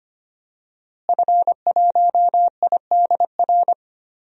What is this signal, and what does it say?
Morse code, heard as a keyed tone, about 25 wpm, 700 Hz. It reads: F1IDR
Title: DXCC-100 Call Signs - 5 Characters - Repeated 3x - 25wpm